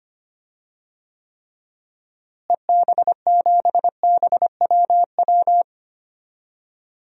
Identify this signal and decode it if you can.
Morse code, heard as a keyed tone, about 25 wpm, 700 Hz. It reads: EB7BWW
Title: DXCC-100 Call Signs - 6 Characters - Repeated 3x - 25wpm